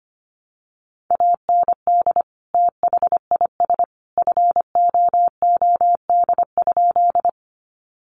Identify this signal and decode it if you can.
Morse code, heard as a keyed tone, about 25 wpm, 700 Hz. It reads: AND THIS FOOD?
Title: Sentences from Top 300 Words 25wpm